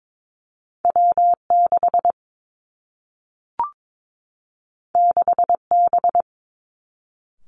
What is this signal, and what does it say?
Morse code, heard as a keyed tone, about 22 wpm, 700 Hz. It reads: W 6 6 B
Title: Two Challenging Letter-Number Combinations 22wpm